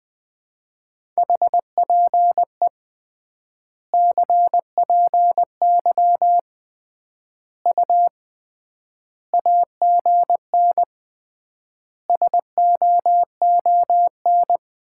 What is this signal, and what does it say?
Morse code, heard as a keyed tone, about 20 wpm, 700 Hz. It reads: HPE CPY U AGN SOON
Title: Common QSO Phrases - 3x Word-Spacing - 20wpm